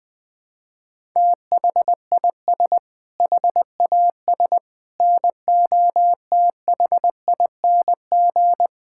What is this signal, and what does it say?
Morse code, heard as a keyed tone, about 20 wpm, 700 Hz. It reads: THIS HAS NOTHING
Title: Sentences with 5-Word Phrases - 20wpm